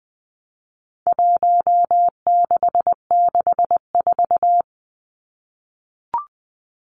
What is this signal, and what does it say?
Morse code, heard as a keyed tone, about 20 wpm, 700 Hz. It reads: 1664
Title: Numbers - Contest Style - 4-Digit 20wpm